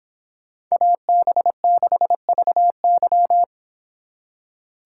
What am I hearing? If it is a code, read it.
Morse code, heard as a keyed tone, about 24 wpm, 700 Hz. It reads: AB6VY